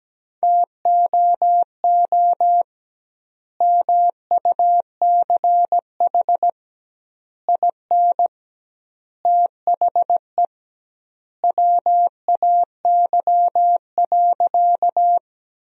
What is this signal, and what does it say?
Morse code, heard as a keyed tone, about 17 wpm, 700 Hz. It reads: TOO MUCH IN THE WAY.
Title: 2x Word-Spacing - Sentences from Top 200 Words - 17wpm